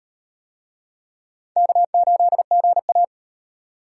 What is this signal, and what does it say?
Morse code, heard as a keyed tone, about 38 wpm, 700 Hz. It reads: K8GA